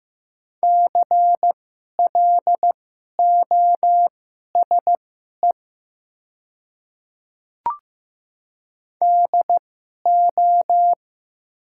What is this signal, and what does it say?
Morse code, heard as a keyed tone, about 15 wpm, spaced as for 11 wpm, 700 Hz. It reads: CLOSE DO
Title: 2x Character Spacing - Top 300 Words - 15wpm